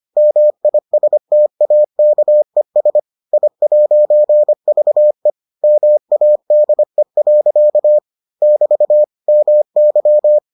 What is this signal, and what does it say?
Morse code, heard as a keyed tone, about 25 wpm, 600 Hz. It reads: MISTAKES I'VE MADE. <BT> MY